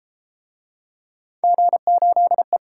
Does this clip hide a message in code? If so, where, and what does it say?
Yes: Morse code from 1.4 to 2.6 s, heard as a keyed tone, about 30 wpm, 700 Hz. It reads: G8E